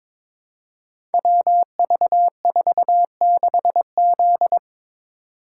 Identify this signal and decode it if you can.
Morse code, heard as a keyed tone, about 22 wpm, 700 Hz. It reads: WV46Z
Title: Code Groups - 5 Letter:Numbers - 22wpm